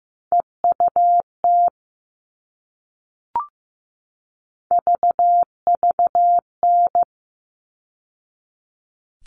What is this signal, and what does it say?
Morse code, heard as a keyed tone, about 15 wpm, 700 Hz. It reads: EUT VVN
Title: All Cut Numbers - 3-Digit 15wpm